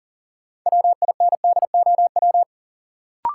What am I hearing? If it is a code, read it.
Morse code, heard as a keyed tone, about 39 wpm, 700 Hz. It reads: WINDOW
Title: Sentences from Top 1600 Words - New Word Drill Alt - 40wpm